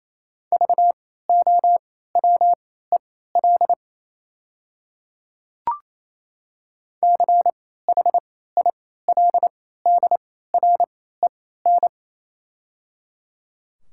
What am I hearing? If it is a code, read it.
Morse code, heard as a keyed tone, about 28 wpm, spaced as for 16 wpm, 700 Hz. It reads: VOWEL CHILDREN
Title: 3x Character Spacing - Top 500 Words - 28wpm